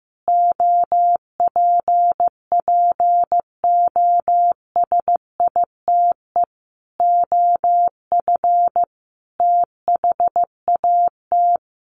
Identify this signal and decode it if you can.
Morse code, heard as a keyed tone, about 15 wpm, 700 Hz. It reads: OPPOSITE OF THAT
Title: Sentences from Top 1300 Words 15wpm